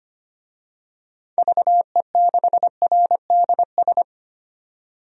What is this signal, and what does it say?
Morse code, heard as a keyed tone, about 25 wpm, 700 Hz. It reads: VE6RDS
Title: DXCC-100 Call Signs - 6 Characters - 25wpm